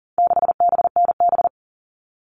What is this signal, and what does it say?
Morse code, heard as a keyed tone, about 40 wpm, 700 Hz. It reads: 6BNB
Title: All Cut Numbers - 4-Digit 40wpm